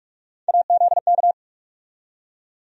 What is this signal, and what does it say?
Morse code, heard as a keyed tone, about 42 wpm, 700 Hz. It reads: AZK
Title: USA Call Sign Postfixes - 3 Letters 45wpm